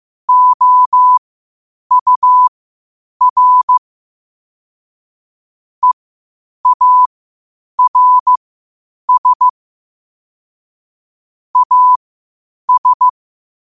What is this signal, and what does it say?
Morse code, heard as a keyed tone, about 15 wpm, spaced as for 8 wpm, 1000 Hz. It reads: OUR EARS AS
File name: qotdfiles / 8wpm